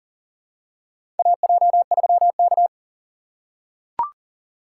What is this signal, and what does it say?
Morse code, heard as a keyed tone, about 40 wpm, 700 Hz. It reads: AJ3K
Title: USA Call Signs - 4 Characters - 6x Character Spacing - 40wpm